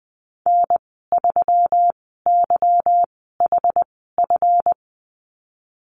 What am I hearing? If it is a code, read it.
Morse code, heard as a keyed tone, about 20 wpm, spaced as for 15 wpm, 700 Hz. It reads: N3YHF
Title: 173 Lesson 28g - Number 9 - Call Signs 20wpm